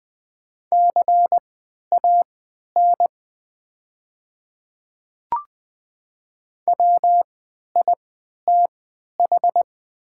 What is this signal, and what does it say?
Morse code, heard as a keyed tone, about 20 wpm, spaced as for 11 wpm, 700 Hz. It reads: CAN WITH
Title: 3x Character Spacing - Top 100 Words - 20wpm